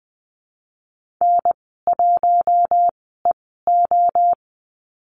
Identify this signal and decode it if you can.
Morse code, heard as a keyed tone, about 20 wpm, spaced as for 14 wpm, 700 Hz. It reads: N1EO
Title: 042 Lesson 8g - Number 1 - Call Signs 20wpm